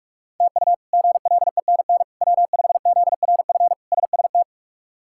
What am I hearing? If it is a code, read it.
Morse code, heard as a keyed tone, about 46 wpm, 700 Hz. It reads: TU GLENN W5ZRF SST
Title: K1USN SST QSO Practice - 45wpm